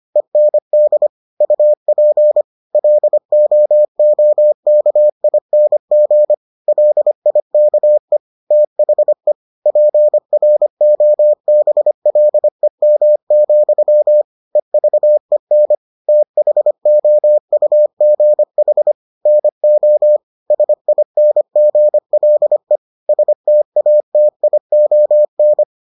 Morse code, heard as a keyed tone, about 25 wpm, 600 Hz. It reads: END UP LOOKING LIKE THE PROBLEM, EVEN THOUGH NO SINGLE STATION